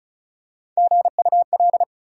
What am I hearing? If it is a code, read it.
Morse code, heard as a keyed tone, about 36 wpm, 700 Hz. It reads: GUL
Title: USA Call Sign Postfixes - 3 Letters 35wpm